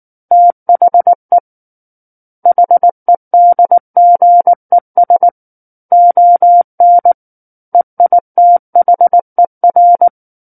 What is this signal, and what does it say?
Morse code, heard as a keyed tone, about 19 wpm, 700 Hz. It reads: THE HEDGES ON EITHER
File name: Book One / 10-Chapter 10 - 19 WPM